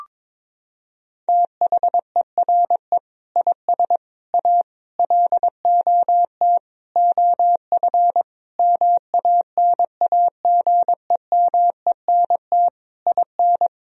Morse code, heard as a keyed tone, about 22 wpm, 700 Hz. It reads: THERE IS A LOT OF MANAGEMENT IN
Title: Sentences from Top 1000 Words - Review Part 4 - 22wpm